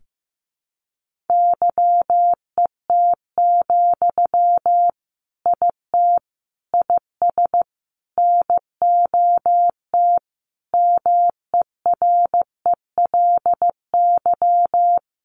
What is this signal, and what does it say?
Morse code, heard as a keyed tone, about 14 wpm, 700 Hz. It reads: YET, IT IS NOT MERELY
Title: Sentences from Top 1400 Words 15wpm